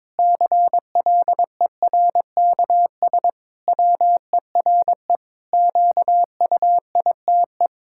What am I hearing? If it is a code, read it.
Morse code, heard as a keyed tone, about 22 wpm, 700 Hz. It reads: CLERKS WERE QUITE
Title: Pangram Challenge v2 - 22wpm